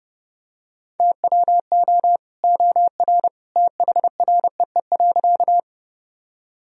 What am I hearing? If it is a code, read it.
Morse code, heard as a keyed tone, about 30 wpm, 700 Hz. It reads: TWO OR THREE.
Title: Sentences from Top 1000 Words - Review Part 1 - 30wpm